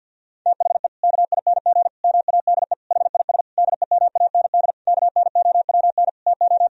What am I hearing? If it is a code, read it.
Morse code, heard as a keyed tone, about 50 wpm, 700 Hz. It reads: THE KING MADE HIS DEMAND KNOWN TO
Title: Sentences from Top 1000 Words - Review Part 4 - 50wpm